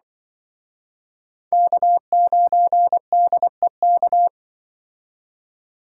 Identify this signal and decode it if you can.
Morse code, heard as a keyed tone, about 25 wpm, 700 Hz. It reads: K9DEK
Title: Speed Racing - USA 1x3 Call Signs 20wpm